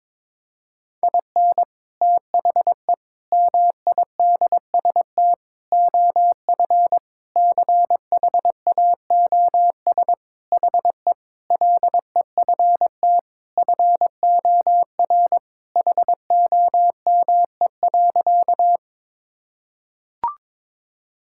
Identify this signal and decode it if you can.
Morse code, heard as a keyed tone, about 22 wpm, 700 Hz. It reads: IN THE MIDST OF CHAOS HE LEFT FOR HOME.
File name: practice / 22wpm / Sentences with 4-Word Phrases - 22wpm